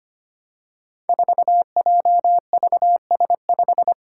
Morse code, heard as a keyed tone, about 25 wpm, 700 Hz. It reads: 4JVS5
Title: Five Challenging Letter-Number Combinations 25wpm